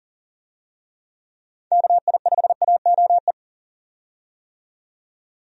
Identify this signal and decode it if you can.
Morse code, heard as a keyed tone, about 37 wpm, 700 Hz. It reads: KI5AOE
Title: USA Call Signs - 6 Characters - 40wpm